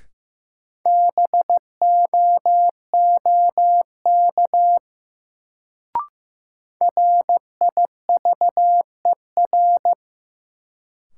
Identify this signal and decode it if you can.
Morse code, heard as a keyed tone, about 15 wpm, 700 Hz. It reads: BOOK RIVER
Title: Sentences from Top 300 Words - New Word Drill - 15wpm